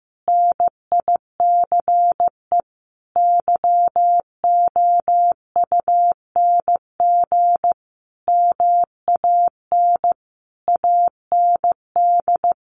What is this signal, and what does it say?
Morse code, heard as a keyed tone, about 15 wpm, 700 Hz. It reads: NICE YOUNG MAN AND
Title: Sentences from Top 1200 Words 15wpm